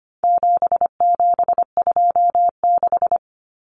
Morse code, heard as a keyed tone, about 25 wpm, 700 Hz. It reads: 7726
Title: Numbers - 4-Digit 25wpm